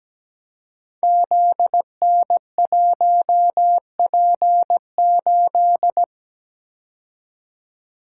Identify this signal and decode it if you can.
Morse code, heard as a keyed tone, about 17 wpm, 690 Hz. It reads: ZN1P8
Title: Code Groups - 5 Letter:Numbers - 17wpm